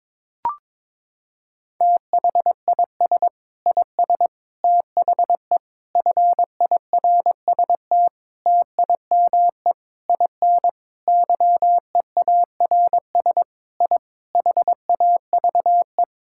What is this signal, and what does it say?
Morse code, heard as a keyed tone, about 22 wpm, 700 Hz. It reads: THIS IS THE FIRST TIME IN YEARS I HAVE
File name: practice / 22wpm / Sentences with 4-Word Phrases - 22wpm